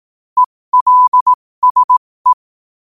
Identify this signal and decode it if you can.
Morse code, heard as a keyed tone, about 18 wpm, spaced as for 15 wpm, 1000 Hz. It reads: ELSE
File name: qotdfiles / 15wpm